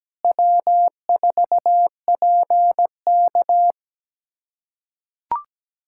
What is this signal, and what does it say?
Morse code, heard as a keyed tone, about 17 wpm, 700 Hz. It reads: W4PK
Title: K1USN SST QSO Practice - 17wpm